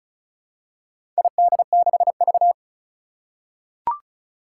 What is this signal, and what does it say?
Morse code, heard as a keyed tone, about 36 wpm, 700 Hz. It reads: ID6V
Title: Four Challenging Letter-Number Combinations 35wpm